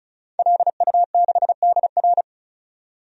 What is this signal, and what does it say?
Morse code, heard as a keyed tone, about 36 wpm, 700 Hz. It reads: LU6DR
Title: DXCC-100 Call Signs - 5 Characters - 35wpm